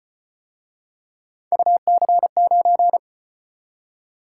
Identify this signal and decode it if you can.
Morse code, heard as a keyed tone, about 33 wpm, 700 Hz. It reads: UC9